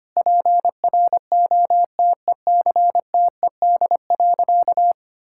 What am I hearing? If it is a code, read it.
Morse code, heard as a keyed tone, about 25 wpm, 700 Hz. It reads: PROTECTED.
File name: practice / 25wpm / Sentences with 9-Word Phrases - 25wpm